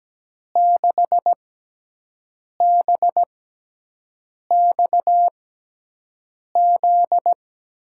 Morse code, heard as a keyed tone, about 17 wpm, 700 Hz. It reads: 6 B X Z
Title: USA Call Signs - 6 Characters - 6x Character Spacing - 17wpm